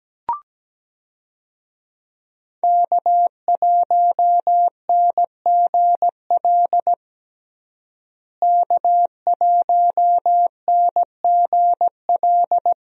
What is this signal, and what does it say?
Morse code, heard as a keyed tone, about 17 wpm, 700 Hz. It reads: K1NGL K1NGL
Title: USA Call Signs - 5 Characters - Repeated 3x - 17wpm